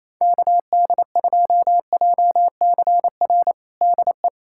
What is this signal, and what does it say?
Morse code, heard as a keyed tone, about 28 wpm, 700 Hz. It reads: KD2JCR DE
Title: USA Callsign DE Callsign 28wpm